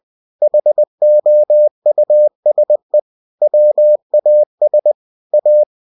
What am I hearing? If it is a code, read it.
Morse code, heard as a keyed tone, about 20 wpm, 600 Hz. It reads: HOUSE WAS A
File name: cw-books / winnie-the-pooh / winnie-20wpm-page-0020